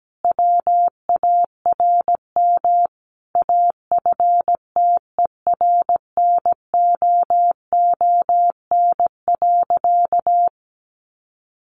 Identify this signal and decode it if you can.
Morse code, heard as a keyed tone, about 17 wpm, 700 Hz. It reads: WARM AFTERNOON.
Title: Sentences from Top 1500 Words 17wpm